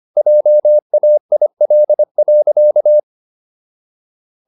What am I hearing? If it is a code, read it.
Morse code, heard as a keyed tone, about 25 wpm, 600 Hz. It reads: JAIL.